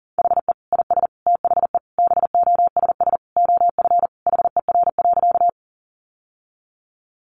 Morse code, heard as a keyed tone, about 40 wpm, 700 Hz. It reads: HE IS THE BOSS OF HER.
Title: Sentences from Top 700 Words 40wpm